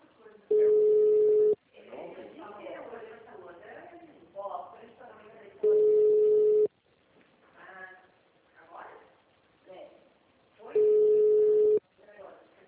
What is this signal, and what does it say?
Ringback tone (Europe, 425 Hz), with speech.